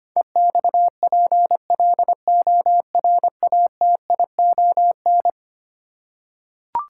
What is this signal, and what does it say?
Morse code, heard as a keyed tone, about 25 wpm, 700 Hz. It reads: EXPLORATION
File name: practice / 25wpm / Eleven Letter Words - 25wpm